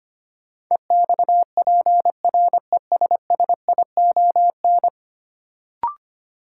Morse code, heard as a keyed tone, about 25 wpm, 700 Hz. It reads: EXPRESSION
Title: Sentences from Top 1200 Words - New Word Drill - 25wpm